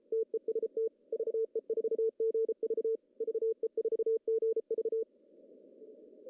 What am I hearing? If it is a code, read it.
Morse code, heard as a keyed tone, about 33 wpm, 450 Hz. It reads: TEST VE4GV VE4GV
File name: CW / cqww-cw-k5tr